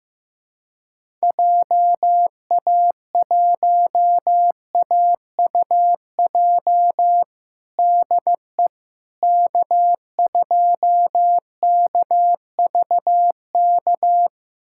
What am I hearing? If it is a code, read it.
Morse code, heard as a keyed tone, about 15 wpm, 700 Hz. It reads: JA1AUJ DE K2KVK
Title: USA Callsign DE Callsign 15wpm